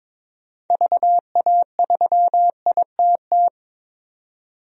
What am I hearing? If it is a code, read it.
Morse code, heard as a keyed tone, about 23 wpm, 700 Hz. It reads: VA3ITT
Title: DXCC-100 Call Signs - 6 Characters - 22wpm